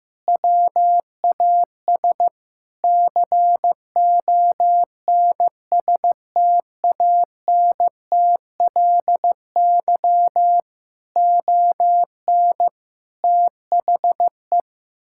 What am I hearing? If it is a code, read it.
Morse code, heard as a keyed tone, about 15 wpm, 700 Hz. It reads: WAS CONSTANTLY ON THE
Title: Sentences from Top 1600 Words 15wpm